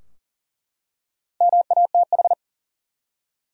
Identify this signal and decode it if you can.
Morse code, heard as a keyed tone, about 40 wpm, 700 Hz. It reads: MATH
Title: Four Letter Words - 40wpm